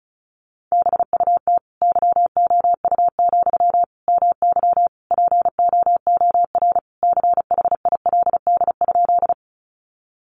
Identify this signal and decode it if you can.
Morse code, heard as a keyed tone, about 34 wpm, 700 Hz. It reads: BUT YOU, MY POOR CHILD?
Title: Sentences from Top 1000 Words 35wpm